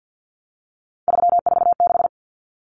Morse code, heard as a keyed tone, about 48 wpm, 700 Hz. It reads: 346
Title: Common Cut Numbers - 3-Digit 50wpm